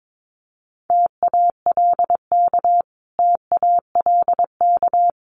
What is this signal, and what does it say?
Morse code, heard as a keyed tone, about 22 wpm, 700 Hz. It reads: TALK TALK
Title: Top 500 Words Repeated 22wpm